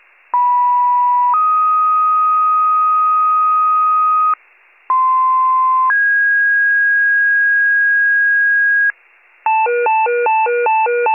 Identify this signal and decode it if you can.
Two-tone page, heard as a one-second tone, then a three-second tone, about 980 then 1250 Hz, about 1040 then 1650 Hz, with noise. It is heard from 0.3 to 8.9 s.